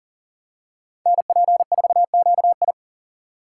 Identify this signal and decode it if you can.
Morse code, heard as a keyed tone, about 40 wpm, 700 Hz. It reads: NP4QI
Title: USA Call Signs - 5 Characters - 6x Character Spacing - 40wpm